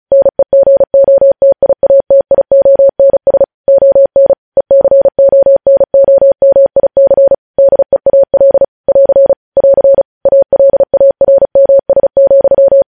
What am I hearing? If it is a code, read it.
Morse code, heard as a keyed tone, about 36 wpm, 550 Hz. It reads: NEGOTIATIONS ON ECONOMIC DEAL <AR> <AR> ALARMS,
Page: News headlines in Morse code at 35 wpm